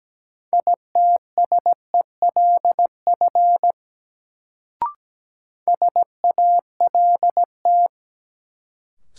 Morse code, heard as a keyed tone, about 17 wpm, 700 Hz. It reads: ITSELF SALT
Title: Sentences from Top 800 Words - New Word Drill - 17wpm